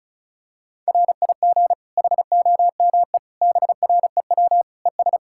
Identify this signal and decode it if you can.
Morse code, heard as a keyed tone, about 36 wpm, 700 Hz. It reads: RIG HOME BREW ES
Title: K1USN SST QSO Practice 35wpm